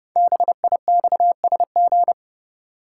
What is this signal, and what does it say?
Morse code, heard as a keyed tone, about 30 wpm, 700 Hz. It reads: BIXSG
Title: Code Groups - 5 Letters - 30wpm